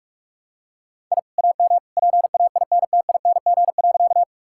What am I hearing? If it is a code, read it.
Morse code, heard as a keyed tone, about 46 wpm, 700 Hz. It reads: I AM PAINTING.
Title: Sentences from Top 1000 Words - Review Part 6 - 45wpm